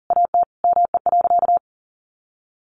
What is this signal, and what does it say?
Morse code, heard as a keyed tone, about 40 wpm, 700 Hz. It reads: AT ME.
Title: Sentences from Top 1500 Words 40wpm